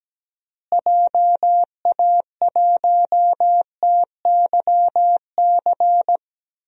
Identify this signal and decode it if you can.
Morse code, heard as a keyed tone, about 17 wpm, 700 Hz. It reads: JA1TYC
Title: DXCC-100 Call Signs - 6 Characters - Repeated 2x - 17wpm